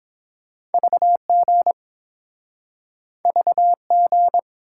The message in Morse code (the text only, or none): VG VG